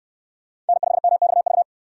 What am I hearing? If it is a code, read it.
Morse code, heard as a keyed tone, about 63 wpm, 700 Hz. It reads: N5MXV